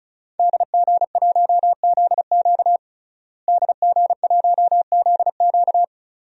Morse code, heard as a keyed tone, about 34 wpm, 700 Hz. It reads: DG1ZQ DG1ZQ
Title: DXCC-100 Call Signs - 5 Characters - Repeated 2x - 35wpm